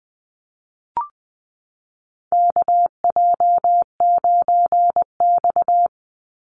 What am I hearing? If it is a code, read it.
Morse code, heard as a keyed tone, about 20 wpm, 700 Hz. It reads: KJ9X
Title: NA SOTA Chaser Callsigns 20wpm